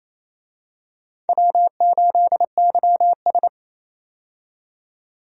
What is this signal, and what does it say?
Morse code, heard as a keyed tone, about 28 wpm, 700 Hz. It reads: W8YS
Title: USA Call Signs - 4 Characters - 28wpm